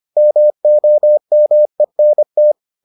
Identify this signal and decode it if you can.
Morse code, heard as a keyed tone, about 25 wpm, 600 Hz. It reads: MOMENT